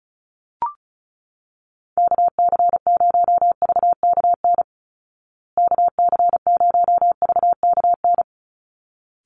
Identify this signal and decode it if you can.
Morse code, heard as a keyed tone, about 34 wpm, 700 Hz. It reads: KC0VKN KC0VKN
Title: Call Signs Repeated 3x - 35wpm